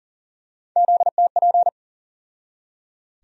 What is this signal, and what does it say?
Morse code, heard as a keyed tone, about 40 wpm, 700 Hz. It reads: ZTP
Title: USA Call Sign Postfixes - 3 Letters 40wpm